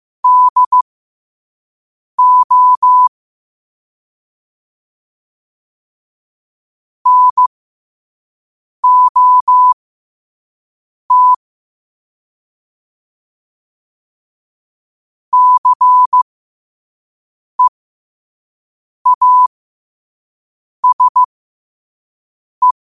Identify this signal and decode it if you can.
Morse code, heard as a keyed tone, about 15 wpm, spaced as for 5 wpm, 1000 Hz. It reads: DO NOT CEASE
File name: qotdfiles / 5wpm